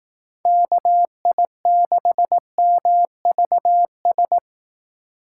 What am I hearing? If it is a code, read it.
Morse code, heard as a keyed tone, about 18 wpm, 700 Hz. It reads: KI6MVS
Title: Speed Racing - USA 2x3 Call Signs 15wpm